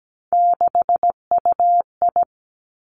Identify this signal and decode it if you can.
Morse code, heard as a keyed tone, about 17 wpm, 700 Hz. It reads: 6UI
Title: Three Challenging Letter-Number Combinations 17wpm